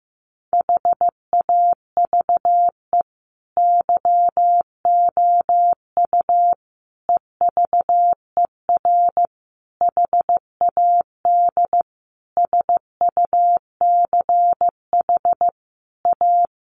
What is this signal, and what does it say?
Morse code, heard as a keyed tone, about 15 wpm, 700 Hz. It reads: HAVE YOU EVER HAD SUCH A
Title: Sentences from Top 1200 Words 15wpm